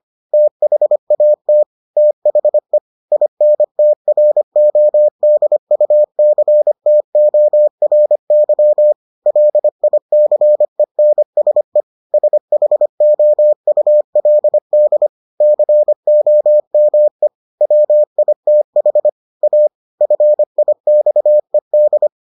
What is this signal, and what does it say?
Morse code, heard as a keyed tone, about 25 wpm, 600 Hz. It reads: THAT THE INTRODUCTORY LICENSE SHOULD COME WITH A FIXED